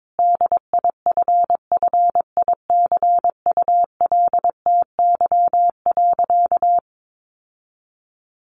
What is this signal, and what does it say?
Morse code, heard as a keyed tone, about 22 wpm, 700 Hz. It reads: DIFFICULTY.
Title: Sentences from Top 1500 Words 22wpm